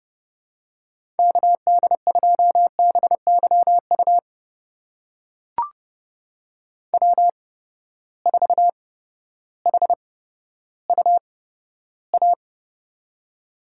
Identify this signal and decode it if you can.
Morse code, heard as a keyed tone, about 32 wpm, spaced as for 14 wpm, 700 Hz. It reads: KD2BYU W4HUA
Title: USA Call Signs - 8x Character Spacing - 30wpm